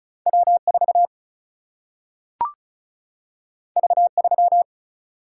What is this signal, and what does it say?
Morse code, heard as a keyed tone, about 36 wpm, 700 Hz. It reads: W4 V3
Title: Single-Letter Single-Number 35wpm